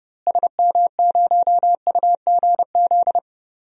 Morse code, heard as a keyed tone, about 30 wpm, 700 Hz. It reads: SM0UGZ
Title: DXCC-100 Call Signs - 6 Characters - Repeated 3x - 30wpm